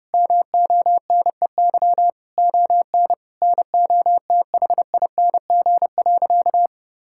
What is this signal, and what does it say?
Morse code, heard as a keyed tone, about 30 wpm, 700 Hz. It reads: MONEY ON NOTHING.
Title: Sentences from Top 1000 Words - Review Part 5 - 30wpm